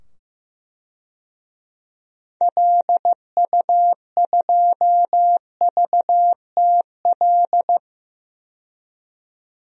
Morse code, heard as a keyed tone, about 15 wpm, 700 Hz. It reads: LU2VTL
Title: DXCC-100 Call Signs - 6 Characters - 15wpm